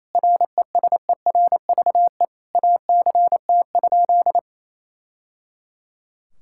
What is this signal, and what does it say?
Morse code, heard as a keyed tone, about 28 wpm, 700 Hz. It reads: RESERVE ACT?
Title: Sentences from Top 1000 Words - Review Part 6 - 28wpm